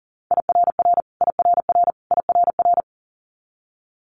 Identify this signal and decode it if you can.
Morse code, heard as a keyed tone, about 40 wpm, 700 Hz. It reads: IRR IRR IRR
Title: Multi-letter Phonemes Repeated 3x 40wpm